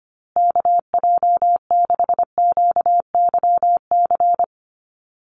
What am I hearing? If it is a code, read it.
Morse code, heard as a keyed tone, about 25 wpm, 700 Hz. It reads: KJ6QYC
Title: USA Call Signs Repeated 2x 25wpm